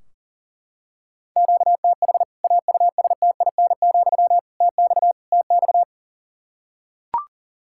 Morse code, heard as a keyed tone, about 40 wpm, 700 Hz. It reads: QTH AUSTIN, TX TX
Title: K1USN SST QSO Practice 40wpm